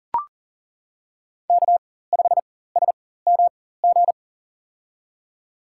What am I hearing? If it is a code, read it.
Morse code, heard as a keyed tone, about 38 wpm, spaced as for 18 wpm, 700 Hz. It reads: K5SMG
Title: USA Call Signs - 5 Characters - 4x Character Spacing - 40wpm